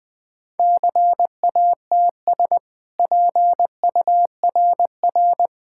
Morse code, heard as a keyed tone, about 20 wpm, 700 Hz. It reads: CATS PURR
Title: Easy Sentences with Hints - 20wpm